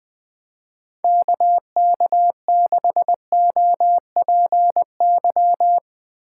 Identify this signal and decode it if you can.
Morse code, heard as a keyed tone, about 20 wpm, 700 Hz. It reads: KK6OPY
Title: Speed Racing - USA 2x3 Call Signs 17wpm